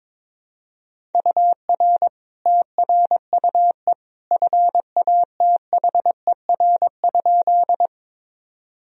Morse code, heard as a keyed tone, about 21 wpm, 700 Hz. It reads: UR TRUE FATHER?
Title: Sentences from Top 400 Words and Abbreviations 22wpm